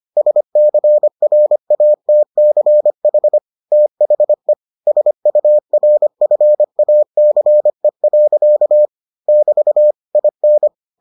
Morse code, heard as a keyed tone, about 25 wpm, 600 Hz. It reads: SCRATCH THE SURFACE. <BT> IN